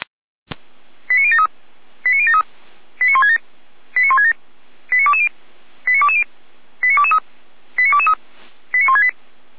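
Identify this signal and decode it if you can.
Five-tone selective call (ZVEI), heard as quick runs of five tones, tones of about 80 ms. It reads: 89073 89073 89167 89167 89220 89220 89223 89223 89168
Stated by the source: Five-tone ZVEI selective call received on SDR.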